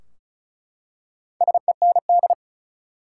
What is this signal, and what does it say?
Morse code, heard as a keyed tone, about 36 wpm, 700 Hz. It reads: SEND
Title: Four Letter Words - 35wpm